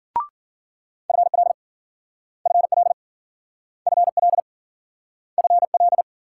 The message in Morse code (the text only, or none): FL FL FL FL